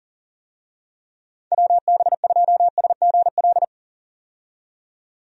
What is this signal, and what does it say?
Morse code, heard as a keyed tone, about 40 wpm, 700 Hz. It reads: WB2SGL
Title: USA Call Signs - 6 Characters - 40wpm